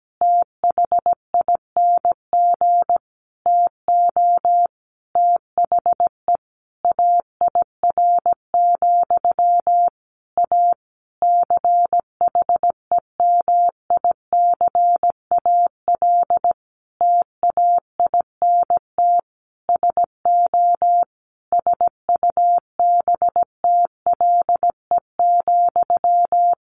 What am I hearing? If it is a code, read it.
Morse code, heard as a keyed tone, about 17 wpm, 700 Hz. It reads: THING TO THE AIR, A CHEMICAL TAINT SO SUBTLE,
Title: A Sound Of Thunder 17wpm